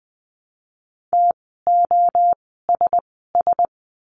Morse code, heard as a keyed tone, about 20 wpm, spaced as for 14 wpm, 700 Hz. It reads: TOSS
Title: 054 Lesson 10f - Letter r - Words 20wpm